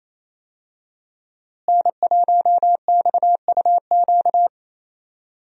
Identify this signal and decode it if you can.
Morse code, heard as a keyed tone, about 28 wpm, 700 Hz. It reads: N1XUQ
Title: USA Call Signs - 5 Characters - Repeated 3x - 28wpm